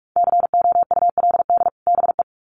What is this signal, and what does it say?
Morse code, heard as a keyed tone, about 44 wpm, 700 Hz. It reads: COULD BE
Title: Sets of 2 Words 45wpm